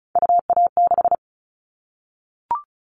Morse code, heard as a keyed tone, about 36 wpm, 690 Hz. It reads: UA6
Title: Callsign Prefixes for Top 50 DXCC - 35wpm